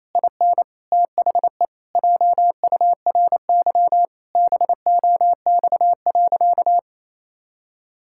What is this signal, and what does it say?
Morse code, heard as a keyed tone, about 28 wpm, 700 Hz. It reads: IN THE JURY BOX.
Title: Pangram Challenge v2 - 28wpm